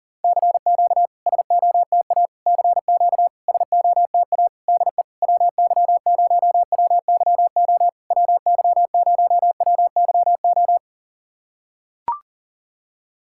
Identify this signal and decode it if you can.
Morse code, heard as a keyed tone, about 40 wpm, 690 Hz. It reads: CQ SOTA CQ SOTA DE WY0WYO WY0WYO
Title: SOTA QSO Practice 40wpm